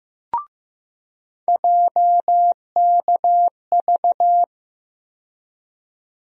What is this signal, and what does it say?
Morse code, heard as a keyed tone, about 15 wpm, 700 Hz. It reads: JKV
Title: USA Call Sign Postfixes - 3 Letters 15wpm